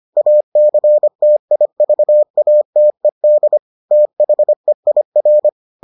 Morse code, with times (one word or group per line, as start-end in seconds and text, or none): 0.2-3.6 ACTIVATED
3.9-5.5 THEIR